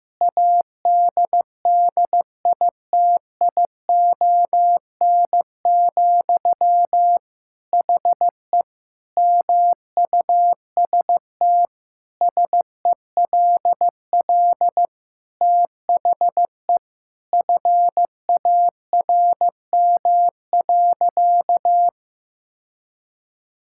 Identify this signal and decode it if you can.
Morse code, heard as a keyed tone, about 15 wpm, 700 Hz. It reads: ADDITION, HE MUST SELL THE FARM.
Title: Sentences from Top 1000 Words - Review Part 6 - 15wpm